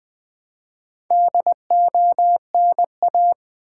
Morse code, heard as a keyed tone, about 20 wpm, 700 Hz. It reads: DONA
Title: Build A Word - 2 Letters At A Time - 8-Letter Words - 20wpm